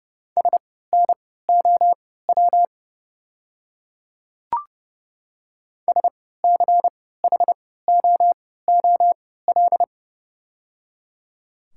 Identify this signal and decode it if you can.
Morse code, heard as a keyed tone, about 30 wpm, spaced as for 17 wpm, 690 Hz. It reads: SNOW SCHOOL
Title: 3x Character Spacing - Top 500 Words - 30wpm